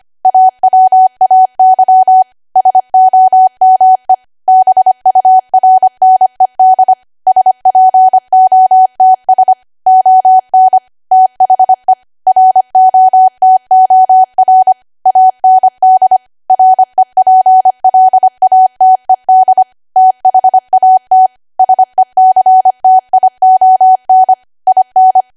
Morse code, heard as a keyed tone, about 25 wpm, 750 Hz. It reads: AWAY SOME BURNED SPOTS ON THE ROTOR AND REPLATED THAT SECTION IN
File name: Archive / 25 WPM / 25WPM